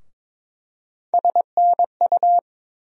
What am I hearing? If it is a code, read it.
Morse code, heard as a keyed tone, about 22 wpm, 700 Hz. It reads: SNU
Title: USA Call Sign Postfixes - 3 Letters 22wpm